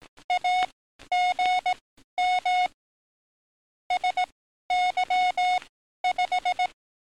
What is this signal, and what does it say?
Morse code, heard as a keyed tone, about 18 wpm, spaced as for 12 wpm, 730 Hz. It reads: AGM SY5
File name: training / NAVPERS 11049to11060 / NAVPERS 11052 RA RB